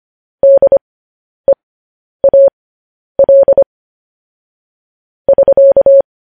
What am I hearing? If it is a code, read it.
Morse code, heard as a keyed tone, about 25 wpm, spaced as for 10 wpm, 550 Hz. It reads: DEAL <SK>
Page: news-headlines-in-morse-code-at-10-wpm